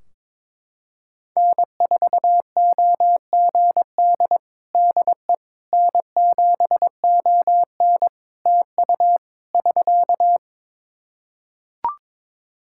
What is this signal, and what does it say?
Morse code, heard as a keyed tone, about 21 wpm, 700 Hz. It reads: N4OGD DE N7ON TU <SK>
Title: Practice QSO - Exchange 4 - Ending - 22wpm